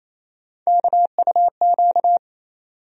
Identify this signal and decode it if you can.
Morse code, heard as a keyed tone, about 28 wpm, 700 Hz. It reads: KUQ
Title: USA Call Sign Postfixes - 3 Letters 28wpm